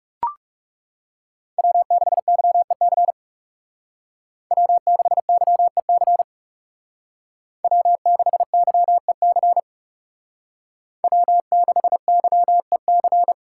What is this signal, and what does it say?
Morse code, heard as a keyed tone, about 36 wpm, 700 Hz. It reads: W6YEC W6YEC W6YEC W6YEC
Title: Speed Racing - USA 1x3 Call Signs 30wpm